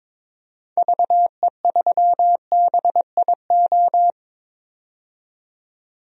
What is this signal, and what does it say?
Morse code, heard as a keyed tone, about 22 wpm, 700 Hz. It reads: VE3BIO
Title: DXCC-100 Call Signs - 6 Characters - 22wpm